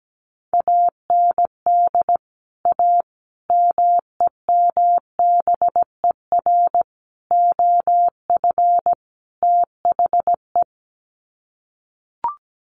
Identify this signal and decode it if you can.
Morse code, heard as a keyed tone, about 17 wpm, 700 Hz. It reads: AND A MEMBER OF THE
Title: Sets of 5 Words 17wpm